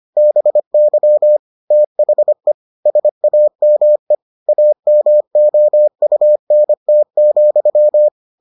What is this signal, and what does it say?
Morse code, heard as a keyed tone, about 25 wpm, 600 Hz. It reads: BY THE SAME AMOUNT,